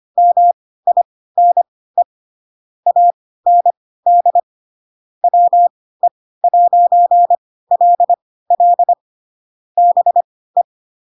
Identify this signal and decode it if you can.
Morse code, heard as a keyed tone, about 25 wpm, spaced as for 16 wpm, 700 Hz. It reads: MINE AND WE'LL BE
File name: Pinocchio / Pinocchio ch2 10 25 ik1hge